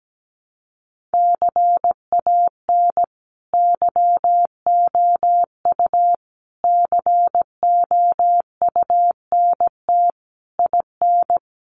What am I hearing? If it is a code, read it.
Morse code, heard as a keyed tone, about 17 wpm, 700 Hz. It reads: CAN YOU COUNT IN
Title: Sentences from Top 1500 Words 17wpm